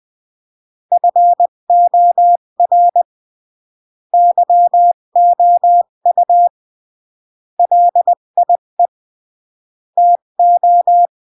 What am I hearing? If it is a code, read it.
Morse code, heard as a keyed tone, about 20 wpm, 700 Hz. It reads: FOR YOU LIE TO
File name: CW / jodorowsky / 82 commandments 20-15